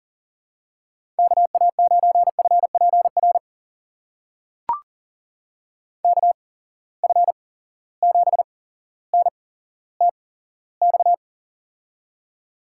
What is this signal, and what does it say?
Morse code, heard as a keyed tone, about 40 wpm, spaced as for 17 wpm, 700 Hz. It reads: KA9FPR KF7NTX
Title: USA Call Signs - 6 Characters - 8x Character Spacing - 40wpm